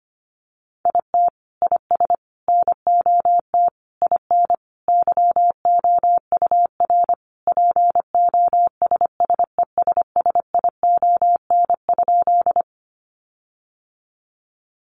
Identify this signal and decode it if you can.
Morse code, heard as a keyed tone, about 25 wpm, 700 Hz. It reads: IT IS NOT IN YOUR POSSESSION?
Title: Sentences from Top 1400 Words 25wpm